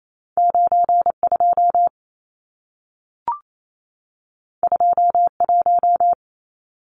Morse code, heard as a keyed tone, about 28 wpm, 700 Hz. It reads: 92 21
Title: Numbers - 2-Digit 28wpm